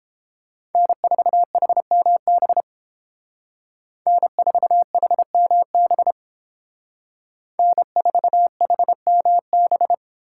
Morse code, heard as a keyed tone, about 28 wpm, 700 Hz. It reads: N4HMB N4HMB N4HMB